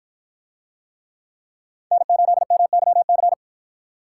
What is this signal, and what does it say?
Morse code, heard as a keyed tone, about 48 wpm, 700 Hz. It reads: N8MYC